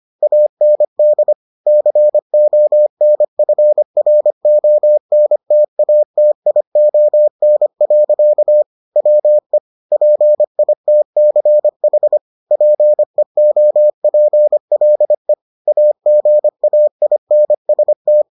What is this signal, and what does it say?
Morse code, heard as a keyed tone, about 25 wpm, 600 Hz. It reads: AND CONFRONTATION. WE PITCH PEOPLE AGAINST